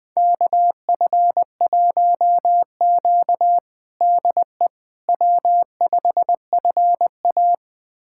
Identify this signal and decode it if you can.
Morse code, heard as a keyed tone, about 20 wpm, 700 Hz. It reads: KF1Q DE W5FA
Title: USA Callsign DE Callsign - 4 Characters - 20wpm